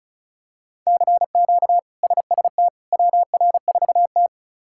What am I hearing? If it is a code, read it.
Morse code, heard as a keyed tone, about 34 wpm, 700 Hz. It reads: CQ SST WR4T
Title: K1USN SST QSO Practice - 35wpm